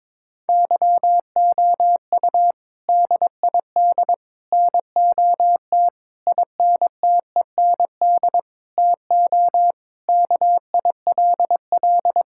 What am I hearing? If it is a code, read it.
Morse code, heard as a keyed tone, about 22 wpm, 700 Hz. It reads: YOU DID NOT INTEND TO KILL
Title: Sentences from Top 1600 Words 22wpm